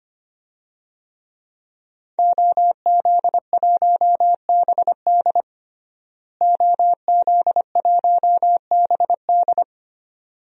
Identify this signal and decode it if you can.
Morse code, heard as a keyed tone, about 25 wpm, 700 Hz. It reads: OZ1BD OZ1BD
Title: DXCC-100 Call Signs - 5 Characters - Repeated 2x - 25wpm